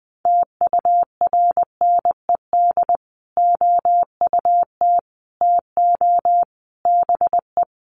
Morse code, heard as a keyed tone, about 20 wpm, 700 Hz. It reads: TURNED OUT TO BE
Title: Sets of 4 Words 20wpm